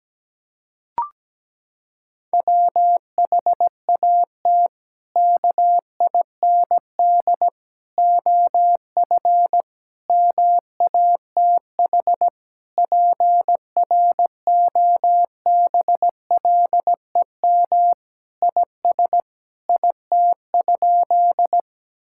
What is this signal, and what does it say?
Morse code, heard as a keyed tone, about 17 wpm, 700 Hz. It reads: WHAT KIND OF MATH PROBLEM IS IT?
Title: Sentences from Top 1000 Words - Review Part 6 - 17wpm